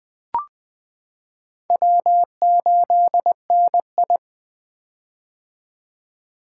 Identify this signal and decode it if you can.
Morse code, heard as a keyed tone, about 20 wpm, 700 Hz. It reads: W8NI
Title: 2023 POTA Activator Call Signs - 20wpm